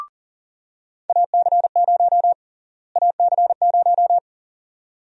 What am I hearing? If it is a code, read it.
Morse code, heard as a keyed tone, about 40 wpm, 700 Hz. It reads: AC0 AC0
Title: USA Call Sign Prefixes - Repeated 2x - 40wpm